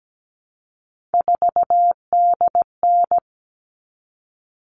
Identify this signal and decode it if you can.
Morse code, heard as a keyed tone, about 17 wpm, 700 Hz. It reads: 4DN
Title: All Cut Numbers - 3-Digit 17wpm